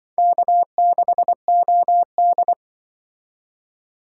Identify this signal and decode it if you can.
Morse code, heard as a keyed tone, about 24 wpm, 700 Hz. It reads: K6OD
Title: Speed Racing - USA 1x2 Call Signs 20wpm